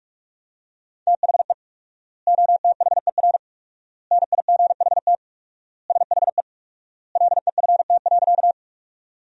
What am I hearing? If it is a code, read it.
Morse code, heard as a keyed tone, about 46 wpm, 700 Hz. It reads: THE OTHER NIGHT SHE LEFT.
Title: 4x Word-Spacing - Sentences from Top 300 Words - 45wpm